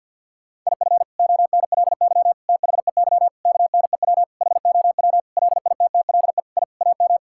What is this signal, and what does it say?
Morse code, heard as a keyed tone, about 50 wpm, 700 Hz. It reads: IF ONLY THEY KNEW HOW LITTLE I AM
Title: Sentences from Top 1000 Words - Review Part 6 - 50wpm